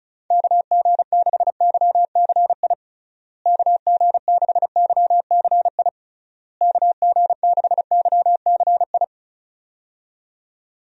Morse code, heard as a keyed tone, about 34 wpm, 700 Hz. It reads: KG6YCI KG6YCI KG6YCI
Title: USA Call Signs - 6 Characters - Repeated 3x - 35wpm